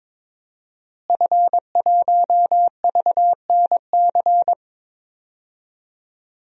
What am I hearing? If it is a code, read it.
Morse code, heard as a keyed tone, about 22 wpm, 700 Hz. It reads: F1VNC
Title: DXCC-100 Call Signs - 5 Characters - Repeated 3x - 22wpm